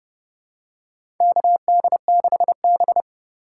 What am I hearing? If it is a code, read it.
Morse code, heard as a keyed tone, about 30 wpm, 700 Hz. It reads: KD6B
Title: USA Call Signs - 4 Characters - 2x Character Spacing - 30wpm